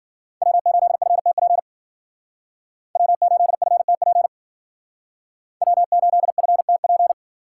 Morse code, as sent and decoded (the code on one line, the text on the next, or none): .-- ---.. ..-. - .--. / .-- ---.. ..-. - .--. / .-- ---.. ..-. - .--.
W8FTP W8FTP W8FTP